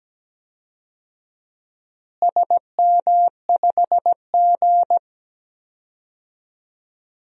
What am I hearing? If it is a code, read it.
Morse code, heard as a keyed tone, about 17 wpm, 700 Hz. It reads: SM5G
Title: DXCC-100 Call Signs - 4 Characters - Repeated 2x - 17wpm